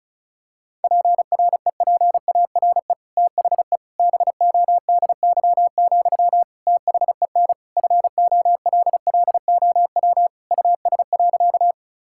Morse code, heard as a keyed tone, about 36 wpm, 700 Hz. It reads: PREPARE THE BODY, THEN FOLLOW US.
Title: Sentences from Top 1000 Words - Review Part 6 - 35wpm